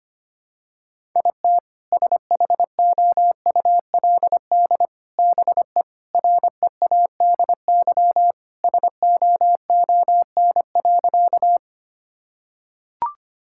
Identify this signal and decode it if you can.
Morse code, heard as a keyed tone, about 25 wpm, 700 Hz. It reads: IT SHOULD BE READY SOON.
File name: practice / 25wpm / Sentences with 2-Word Phrases - 25wpm